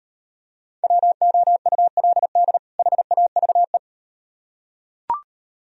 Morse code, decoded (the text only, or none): WOULD HAVE